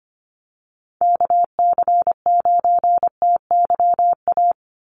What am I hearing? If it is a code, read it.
Morse code, heard as a keyed tone, about 25 wpm, 700 Hz. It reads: KC9TYA
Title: NA SOTA Chaser Callsigns 25wpm